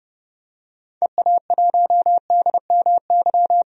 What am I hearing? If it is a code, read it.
Morse code, heard as a keyed tone, about 31 wpm, 700 Hz. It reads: EA1DMY
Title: DXCC-100 Call Signs - 6 Characters - 30wpm